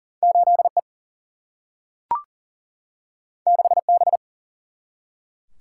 Morse code, heard as a keyed tone, about 40 wpm, 700 Hz. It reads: 8 E 6 B
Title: Single-Letter Single-Number 40wpm